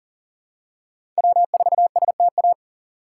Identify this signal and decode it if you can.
Morse code, heard as a keyed tone, about 40 wpm, 700 Hz. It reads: W4STA
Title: USA Call Signs - 5 Characters - 2x Character Spacing - 40wpm